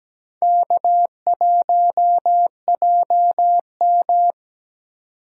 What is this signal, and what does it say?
Morse code, heard as a keyed tone, about 17 wpm, 700 Hz. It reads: K1JM
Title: USA Call Signs - 3x Character Spacing - 17wpm